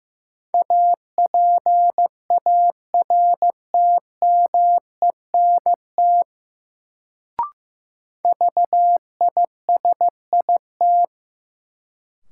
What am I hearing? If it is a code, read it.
Morse code, heard as a keyed tone, about 15 wpm, 700 Hz. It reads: APARTMENT VISIT
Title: Sentences from Top 900 Words - New Word Drill - 15wpm